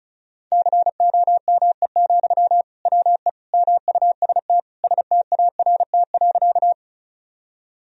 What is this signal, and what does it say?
Morse code, heard as a keyed tone, about 34 wpm, 700 Hz. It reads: COME, WE MUST START.
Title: Sentences from Top 1000 Words - Review Part 1 - 35wpm